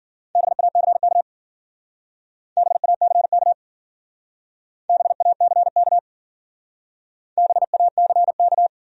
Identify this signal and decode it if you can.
Morse code, heard as a keyed tone, about 50 wpm, 700 Hz. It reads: BACK BACK BACK BACK
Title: Speed-Racing Top 200 Words 40wpm